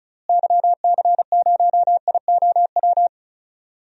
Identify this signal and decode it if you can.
Morse code, heard as a keyed tone, about 36 wpm, 700 Hz. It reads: YC0IOW
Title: DXCC-100 Call Signs - 6 Characters - Repeated 3x - 35wpm